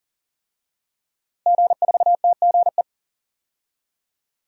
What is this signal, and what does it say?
Morse code, heard as a keyed tone, about 40 wpm, 700 Hz. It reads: G4TGE
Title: DXCC-100 Call Signs - 5 Characters - Repeated 3x - 40wpm